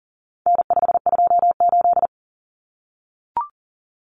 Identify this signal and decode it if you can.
Morse code, heard as a keyed tone, about 40 wpm, 700 Hz. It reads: N528
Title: Common Cut Numbers - 4-Digit 40wpm